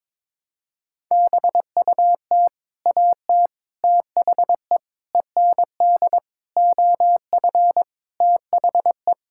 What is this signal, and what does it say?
Morse code, heard as a keyed tone, about 21 wpm, 700 Hz. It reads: BUT AT THE END OF THE
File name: practice / 22wpm / Sentences with 8-Word Phrases - 22wpm